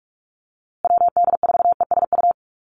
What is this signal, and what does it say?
Morse code, heard as a keyed tone, about 44 wpm, 690 Hz. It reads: WD4ESU
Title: USA Call Signs Repeated 3x 45wpm